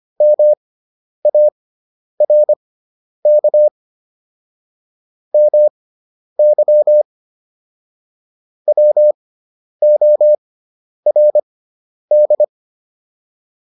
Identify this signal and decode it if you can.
Morse code, heard as a keyed tone, about 25 wpm, spaced as for 10 wpm, 600 Hz. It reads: MARK MY WORD